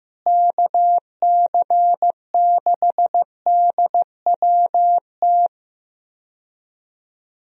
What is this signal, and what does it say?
Morse code, heard as a keyed tone, about 15 wpm, 700 Hz. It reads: KC6DWT
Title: USA Call Signs - 6 Characters - Repeated 2x - 15wpm